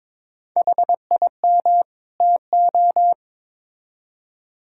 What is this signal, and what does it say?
Morse code, heard as a keyed tone, about 22 wpm, 700 Hz. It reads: HIM TO
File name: Build A Phrase - 2-Word Phrases - 22wpm